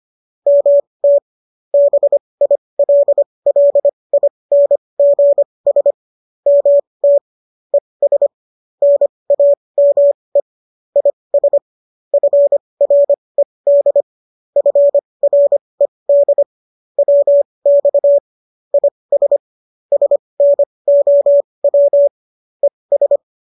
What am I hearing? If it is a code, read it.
Morse code, heard as a keyed tone, about 25 wpm, spaced as for 20 wpm, 570 Hz. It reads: MT BILLINGS MT ES NAME IS FRED FRED WX IS SNOW ES